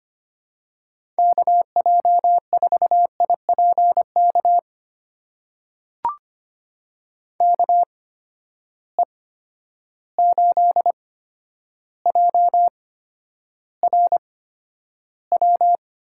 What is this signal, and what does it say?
Morse code, heard as a keyed tone, about 25 wpm, spaced as for 11 wpm, 700 Hz. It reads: KJ4IPK KE8JRW
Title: USA Call Signs - 8x Character Spacing - 25wpm